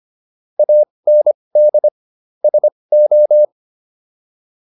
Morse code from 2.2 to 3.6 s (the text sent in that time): SO